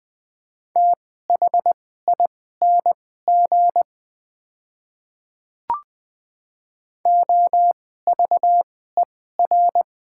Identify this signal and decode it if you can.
Morse code, heard as a keyed tone, about 20 wpm, spaced as for 14 wpm, 700 Hz. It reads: THING OVER
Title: 2x Character Spacing - Top 200 Words - 20wpm